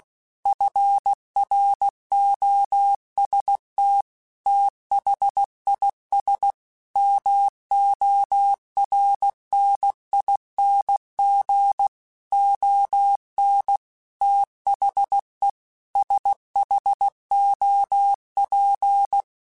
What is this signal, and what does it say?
Morse code, heard as a keyed tone, about 16 wpm, 800 Hz. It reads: FROST THIS MORNING ON THE SHOP
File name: Weather Sets / Weather Sets-mp3s / Weather 4 28 @15